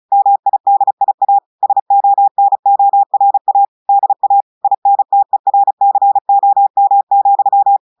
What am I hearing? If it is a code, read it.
Morse code, heard as a keyed tone, about 36 wpm, 800 Hz. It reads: MIDIA SONORA DA INTERCOM,
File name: audios / portugais / portugais texte en clair 1 vitesse 35 mots minutes